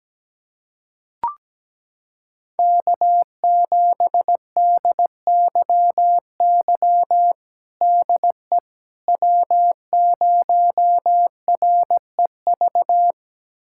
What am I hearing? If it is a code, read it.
Morse code, heard as a keyed tone, about 17 wpm, 700 Hz. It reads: K7DYY DE W0REV
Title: USA Callsign DE Callsign - 5 Characters - 17wpm